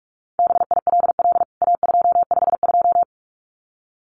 Morse code, heard as a keyed tone, about 44 wpm, 700 Hz. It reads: BILL A252
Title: CWT Exchange 45wpm